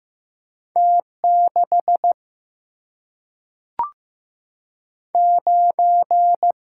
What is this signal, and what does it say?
Morse code, heard as a keyed tone, about 15 wpm, 700 Hz. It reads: T 6 9
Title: Single-Letter Single-Number 15wpm